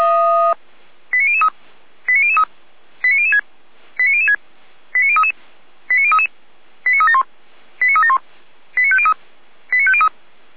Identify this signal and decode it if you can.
Five-tone selective call (ZVEI), heard as quick runs of five tones, tones of about 80 ms. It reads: 89003 89003 89006 89006 89033 89033 89371 89371 89503 89503